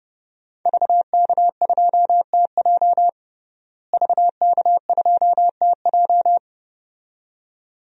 Morse code, heard as a keyed tone, about 30 wpm, 700 Hz. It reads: VK2TJ VK2TJ
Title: DXCC-100 Call Signs - 5 Characters - Repeated 2x - 30wpm